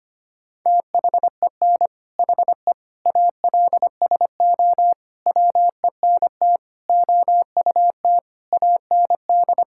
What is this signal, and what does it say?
Morse code, heard as a keyed tone, about 25 wpm, 700 Hz. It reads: THEN HE ALSO WENT OUT AND
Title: Sentences from Top 1600 Words 25wpm